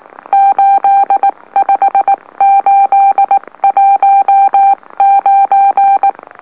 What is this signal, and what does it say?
Morse code, heard as a keyed tone, about 19 wpm, spaced as for 17 wpm, 790 Hz. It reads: 85819
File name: M04- -2345utc-LOLO